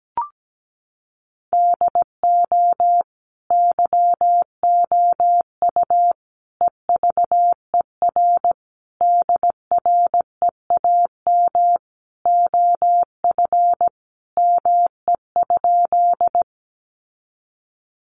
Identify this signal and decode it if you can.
Morse code, heard as a keyed tone, about 17 wpm, 700 Hz. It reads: DO YOU EVER DREAM OF ME?
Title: Sentences from Top 1500 Words 17wpm